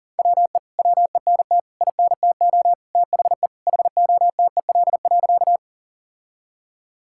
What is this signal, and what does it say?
Morse code, heard as a keyed tone, about 40 wpm, 700 Hz. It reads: WE WENT INTO THE HOTEL.
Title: Sentences from Top 1600 Words 40wpm